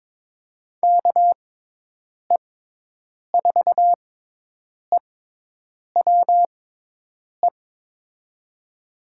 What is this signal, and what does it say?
Morse code, heard as a keyed tone, about 22 wpm, spaced as for 8 wpm, 700 Hz. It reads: KE4EWE
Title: USA Call Signs - 6 Characters - 6x Character Spacing - 22wpm